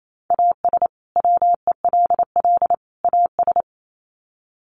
Morse code, heard as a keyed tone, about 28 wpm, 700 Hz. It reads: AS WELL AS
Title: Sets of 3 Words 28wpm